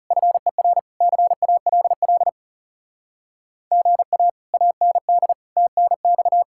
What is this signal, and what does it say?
Morse code, heard as a keyed tone, about 37 wpm, 700 Hz. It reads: FER CALL GA AND TNX
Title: Speed-Racing Common QSO Phrases - 30wpm